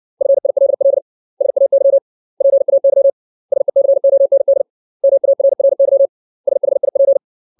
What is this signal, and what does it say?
Morse code, heard as a keyed tone, about 60 wpm, 550 Hz. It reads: FILL SAY PAY SECOND MARRY SHIP
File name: 632 words / 632 Words 60WPM